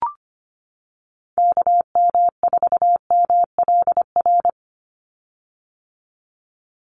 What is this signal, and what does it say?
Morse code, heard as a keyed tone, about 25 wpm, 700 Hz. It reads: KM4MLR
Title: USA Call Signs 25wpm